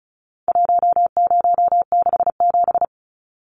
Morse code, heard as a keyed tone, about 34 wpm, 700 Hz. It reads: 1067
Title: Numbers - 4-Digit 35wpm